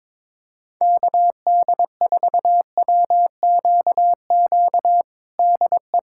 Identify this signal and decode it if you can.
Morse code, heard as a keyed tone, about 22 wpm, 700 Hz. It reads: KD4WQQ DE
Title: USA Callsign DE Callsign - 6 Characters - 22wpm